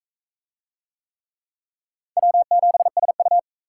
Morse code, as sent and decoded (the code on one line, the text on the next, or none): .-- --... ... ..-
W7SU